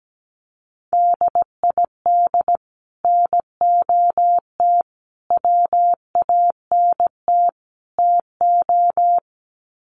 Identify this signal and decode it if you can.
Morse code, heard as a keyed tone, about 17 wpm, 700 Hz. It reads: DID NOT WANT TO
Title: Sets of 4 Words 17wpm